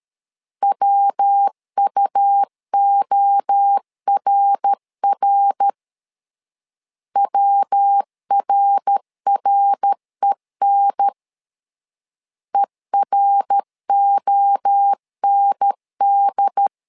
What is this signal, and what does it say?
Morse code, heard as a keyed tone, about 13 wpm, 800 Hz. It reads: WUORR WRREN EROND
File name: LEZIONI / 3 MWRO / 3 MWRO 12wpm 1730